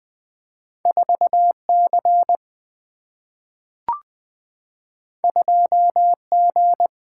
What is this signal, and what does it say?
Morse code, heard as a keyed tone, about 20 wpm, 700 Hz. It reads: 4C 2G
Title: Two Letter-Number Combinations - 20wpm